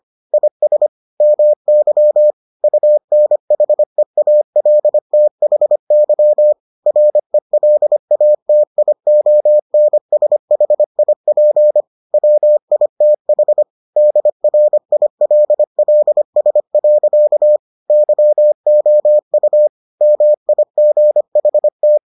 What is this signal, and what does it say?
Morse code, heard as a keyed tone, about 25 wpm, 600 Hz. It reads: IS MY UNHEALTHY RELATIONSHIP WITH DRILLS. YOU MIGHT